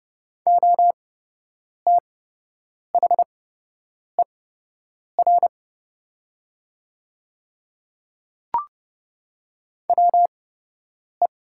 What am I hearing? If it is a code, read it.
Morse code, heard as a keyed tone, about 29 wpm, spaced as for 8 wpm, 700 Hz. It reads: OTHER WE
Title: 8x Character Spacing - Top 100 Words - 30wpm